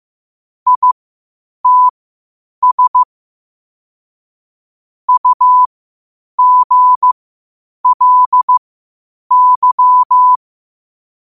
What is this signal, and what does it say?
Morse code, heard as a keyed tone, about 15 wpm, spaced as for 8 wpm, 1000 Hz. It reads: ITS UGLY
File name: qotdfiles / 8wpm